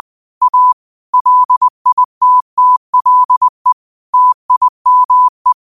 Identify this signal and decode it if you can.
Morse code, heard as a keyed tone, about 20 wpm, 1000 Hz. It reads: A LITTLE TIME